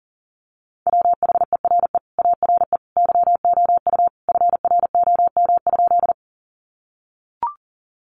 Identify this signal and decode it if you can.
Morse code, heard as a keyed tone, about 40 wpm, 700 Hz. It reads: WHERE ARE YOU FROM?
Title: Sentences from Top 200 Words 40wpm